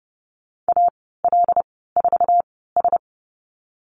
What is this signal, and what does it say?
Morse code, heard as a keyed tone, about 30 wpm, spaced as for 17 wpm, 700 Hz. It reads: AL4S
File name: Lesson 14g - Number 2 - Call Signs 30wpm